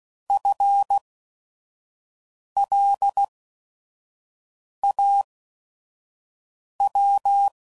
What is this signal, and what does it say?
Morse code, heard as a keyed tone, about 16 wpm, spaced as for 5 wpm, 800 Hz. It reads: FLAW
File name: Prefixes / By First Letter / Fs